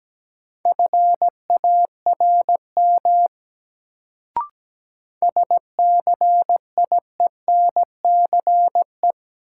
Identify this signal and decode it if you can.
Morse code, heard as a keyed tone, about 17 wpm, 700 Hz. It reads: FARM SCIENCE
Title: Sentences from Top 1000 Words - Review Part 2 - New Word Drill - 17wpm